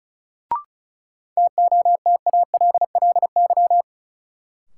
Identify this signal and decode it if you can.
Morse code, heard as a keyed tone, about 34 wpm, 700 Hz. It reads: TOTALLY
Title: Sentences from Top 1600 Words - New Word Drill - 35wpm